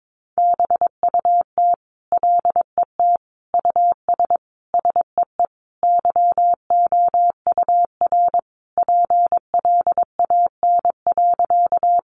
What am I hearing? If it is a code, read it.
Morse code, heard as a keyed tone, about 22 wpm, 700 Hz. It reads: BUT LET US SEE YOUR PLAN.
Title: Sentences from Top 500 Words 22wpm